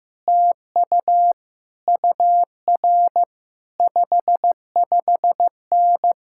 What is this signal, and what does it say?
Morse code, heard as a keyed tone, about 15 wpm, 700 Hz. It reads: TU UR 55N
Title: Speed-Racing Common QSO Phrases Encore - 15wpm